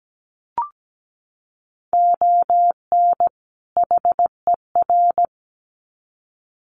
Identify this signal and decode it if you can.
Morse code, heard as a keyed tone, about 17 wpm, 700 Hz. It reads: ON HER
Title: Sets of 2 Words 17wpm